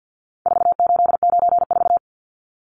Morse code, heard as a keyed tone, about 50 wpm, 700 Hz. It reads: 4894
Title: Numbers - 4-Digit 50wpm